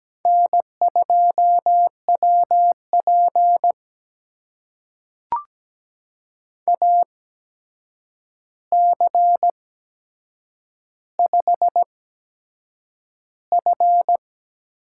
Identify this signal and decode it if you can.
Morse code, heard as a keyed tone, about 17 wpm, spaced as for 7 wpm, 700 Hz. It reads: N2WP AC5F
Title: USA Call Signs - 4 Characters - 8x Character Spacing - 17wpm